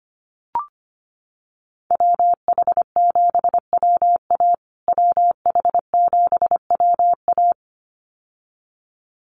Morse code, heard as a keyed tone, about 25 wpm, 700 Hz. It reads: WH7WA WH7WA
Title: USA Call Signs Repeated 2x 25wpm